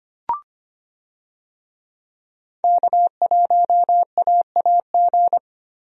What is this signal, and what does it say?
Morse code, heard as a keyed tone, about 25 wpm, 700 Hz. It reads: K1AAG
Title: USA Call Signs - 5 Characters - Repeated 3x - 25wpm